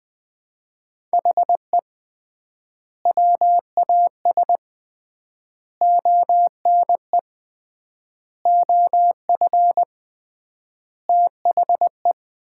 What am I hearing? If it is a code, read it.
Morse code, heard as a keyed tone, about 20 wpm, 700 Hz. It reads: HE WAS ONE OF THE
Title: 3x Word-Spacing - Sets of 5 Words - 20wpm